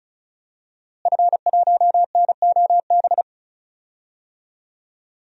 Morse code, heard as a keyed tone, about 36 wpm, 700 Hz. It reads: F1NOB
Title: DXCC-100 Call Signs - 5 Characters - Repeated 3x - 35wpm